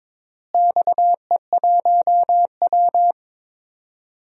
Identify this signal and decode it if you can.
Morse code, heard as a keyed tone, about 22 wpm, 700 Hz. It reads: XE1W
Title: DXCC-100 Call Signs - 4 Characters - Repeated 2x - 22wpm